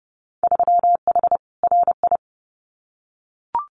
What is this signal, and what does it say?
Morse code, heard as a keyed tone, about 30 wpm, 700 Hz. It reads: 3H RI
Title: Winter Field Day Exchanges Encore 30wpm